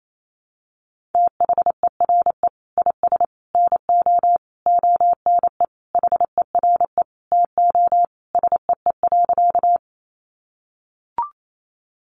Morse code, heard as a keyed tone, about 28 wpm, 700 Hz. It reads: THERE IS NO ONE HERE TO SEE.
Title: Sentences from Top 200 Words 28wpm